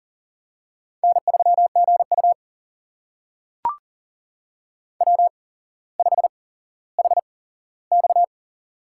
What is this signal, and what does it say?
Morse code, heard as a keyed tone, about 40 wpm, spaced as for 17 wpm, 700 Hz. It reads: N3GU W5HX
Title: USA Call Signs - 4 Characters - 8x Character Spacing - 40wpm